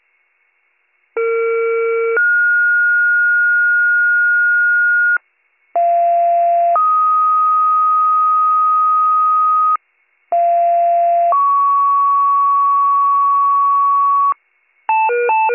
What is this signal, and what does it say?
Two-tone page, heard as a one-second tone, then a three-second tone, about 460 then 1430 Hz, about 690 then 1250 Hz, about 690 then 1120 Hz. It is heard from 1.2 to 14.3 s.